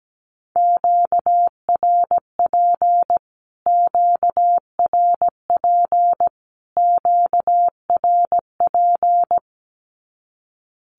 Morse code, heard as a keyed tone, about 17 wpm, 700 Hz. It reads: QRP QRP QRP
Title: Q-Codes Repeated 3x 17wpm